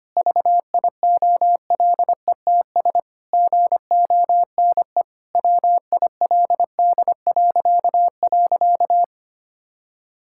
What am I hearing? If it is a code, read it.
Morse code, heard as a keyed tone, about 25 wpm, 700 Hz. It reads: VIOLETS GONE WILD..
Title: Pangram Challenge v2 - 25wpm